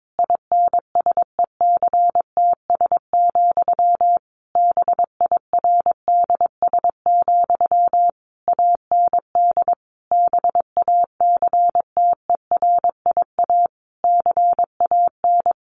Morse code, heard as a keyed tone, about 22 wpm, 700 Hz. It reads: INSECTS, BIRDS, AND BACTERIA CAN
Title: A Sound Of Thunder 22wpm